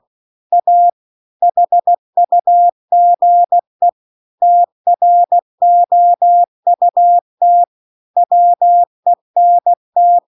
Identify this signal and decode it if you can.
Morse code, heard as a keyed tone, about 16 wpm, 700 Hz. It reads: A HUGE TROUT WENT
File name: CW-The-Big-Twoheaded-River-2-16WPM